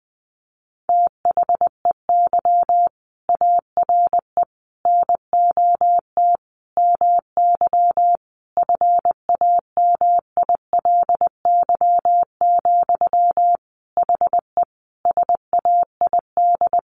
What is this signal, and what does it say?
Morse code, heard as a keyed tone, about 20 wpm, 700 Hz. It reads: THEY ARE NOT MY FAMILY, HE SAID
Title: Sentences from Top 1100 Words 20wpm